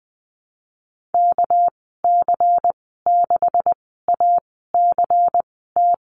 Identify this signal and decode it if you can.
Morse code, heard as a keyed tone, about 20 wpm, spaced as for 14 wpm, 700 Hz. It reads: KC6ACT
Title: 219 Lesson 35g - Number 0 - Call Signs 20wpm